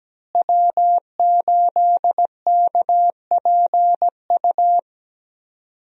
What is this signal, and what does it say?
Morse code, heard as a keyed tone, about 17 wpm, 700 Hz. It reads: W8KPU
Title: USA Call Signs - 5 Characters - Repeated 2x - 17wpm